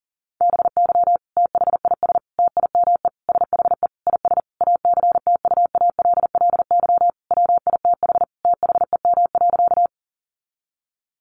Morse code, heard as a keyed tone, about 40 wpm, 700 Hz. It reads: BY THIS TIME SHE IS ACTUALLY WITH THEM.
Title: Sentences from Top 700 Words 40wpm